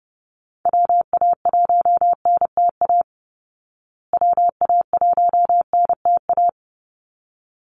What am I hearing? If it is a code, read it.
Morse code, heard as a keyed tone, about 30 wpm, 700 Hz. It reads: WA1NTA WA1NTA
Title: Call Signs Repeated 3x - 30wpm